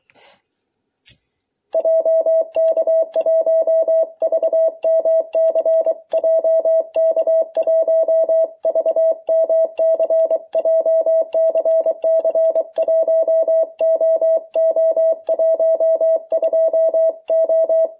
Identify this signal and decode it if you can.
Morse code, heard as a keyed tone, about 24 wpm, 660 Hz. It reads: JK1VMC JK1VMC JCC 1OO12O